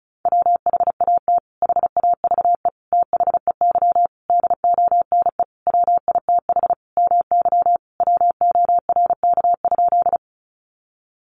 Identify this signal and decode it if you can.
Morse code, heard as a keyed tone, about 34 wpm, 700 Hz. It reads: WHAT HAVE THEY DONE WITH MY WORK?
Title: Sentences from Top 500 Words 35wpm